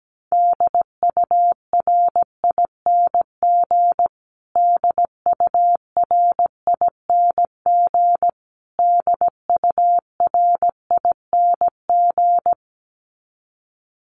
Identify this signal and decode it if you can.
Morse code, heard as a keyed tone, about 17 wpm, 700 Hz. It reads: DURING DURING DURING
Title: Top 500 Words Repeated 17wpm